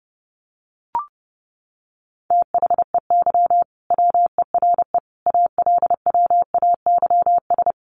Morse code, heard as a keyed tone, about 30 wpm, 700 Hz. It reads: THEY WERE ALWAYS
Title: Sentences from Top 1400 Words 30wpm